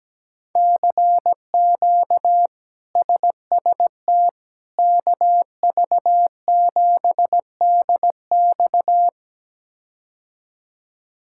Morse code, heard as a keyed tone, about 17 wpm, 700 Hz. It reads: CQ SST KV7DX
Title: K1USN SST QSO Practice - 17wpm